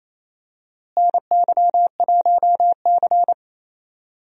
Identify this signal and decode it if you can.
Morse code, heard as a keyed tone, about 28 wpm, 700 Hz. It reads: NY1C
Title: USA Call Signs - 4 Characters - Repeated 2x - 28wpm